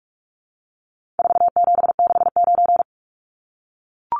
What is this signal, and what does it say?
Morse code, heard as a keyed tone, about 44 wpm, 700 Hz. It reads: 4769
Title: Numbers - 4-Digit 45wpm